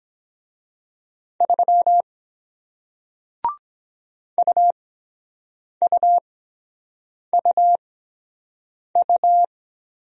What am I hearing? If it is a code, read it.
Morse code, heard as a keyed tone, about 23 wpm, spaced as for 7 wpm, 700 Hz. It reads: 3 UUUU